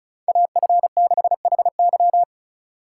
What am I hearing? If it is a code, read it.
Morse code, heard as a keyed tone, about 44 wpm, 700 Hz. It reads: AF6HY